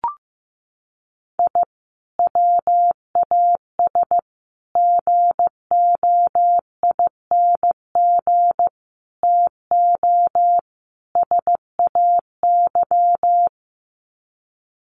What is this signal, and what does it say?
Morse code, heard as a keyed tone, about 15 wpm, 700 Hz. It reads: I WAS GOING TO SAY
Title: Sets of 5 Words 15wpm